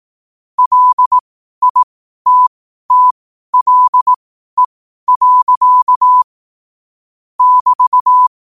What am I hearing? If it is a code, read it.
Morse code, heard as a keyed tone, about 18 wpm, spaced as for 12 wpm, 1000 Hz. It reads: LITTLE. <BT>
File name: qotdfiles / 12wpm